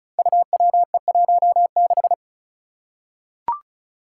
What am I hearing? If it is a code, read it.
Morse code, heard as a keyed tone, about 36 wpm, 700 Hz. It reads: UWE16
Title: Five Challenging Letter-Number Combinations 35wpm